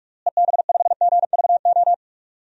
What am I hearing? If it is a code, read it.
Morse code, heard as a keyed tone, about 42 wpm, 700 Hz. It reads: EB5GVO
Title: DXCC-100 Call Signs - 6 Characters - Repeated 3x - 45wpm